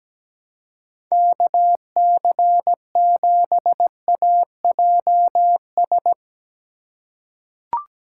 Morse code, heard as a keyed tone, about 17 wpm, 700 Hz. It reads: KC7AJS
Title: USA Call Signs - 6 Characters - 6x Character Spacing - 17wpm